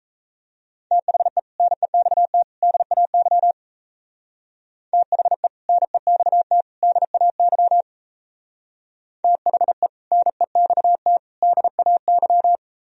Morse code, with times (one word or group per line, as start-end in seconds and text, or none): 0.9-1.4 THE
1.6-2.4 NEXT
2.6-3.5 DAY
4.9-5.5 THE
5.7-6.6 NEXT
6.8-7.8 DAY
9.2-9.9 THE
10.1-11.2 NEXT
11.4-12.6 DAY